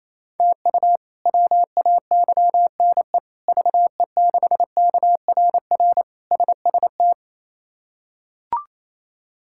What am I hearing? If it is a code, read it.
Morse code, heard as a keyed tone, about 28 wpm, 700 Hz. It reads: TU WAYNE VE6KRR SST
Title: K1USN SST QSO Practice - 28wpm